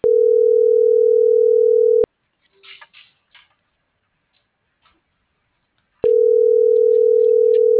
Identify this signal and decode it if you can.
Ringback tone (North America), 440 + 480 Hz.